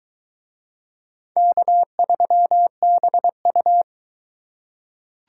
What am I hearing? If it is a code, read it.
Morse code, heard as a keyed tone, about 22 wpm, 700 Hz. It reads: K3BU